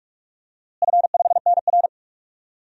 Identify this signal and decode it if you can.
Morse code, heard as a keyed tone, about 46 wpm, 700 Hz. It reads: F5NR
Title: DXCC-100 Call Signs - 4 Characters - Repeated 3x - 45wpm